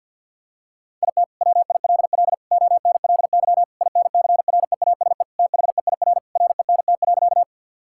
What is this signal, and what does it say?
Morse code, heard as a keyed tone, about 50 wpm, 700 Hz. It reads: IT WILL ONLY INCREASE THEIR RENT.
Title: Sentences from Top 1000 Words - Review Part 5 - 50wpm